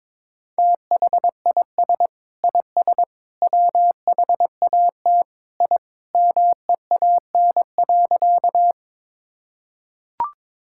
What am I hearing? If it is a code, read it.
Morse code, heard as a keyed tone, about 22 wpm, 700 Hz. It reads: THIS IS WHAT I MEAN.
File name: practice / 22wpm / Sentences with 3-Word Phrases - 22wpm